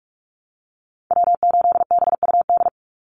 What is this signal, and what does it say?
Morse code, heard as a keyed tone, about 44 wpm, 700 Hz. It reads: W8BUD
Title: Call Signs Repeated 3x - 45wpm